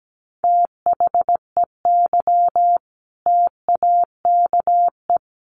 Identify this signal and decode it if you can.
Morse code, heard as a keyed tone, about 17 wpm, 700 Hz. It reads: THEY TAKE
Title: Sentences from Top 900 Words 17wpm